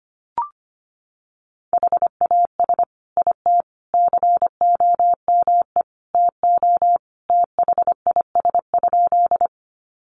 Morse code, heard as a keyed tone, about 25 wpm, 700 Hz. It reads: HAS IT COME TO THIS?
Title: Sentences from Top 100 Words 25wpm